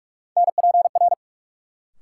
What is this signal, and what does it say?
Morse code, heard as a keyed tone, about 44 wpm, 700 Hz. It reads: NPR
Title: Sets of 5 Words - New Word Drill - 45wpm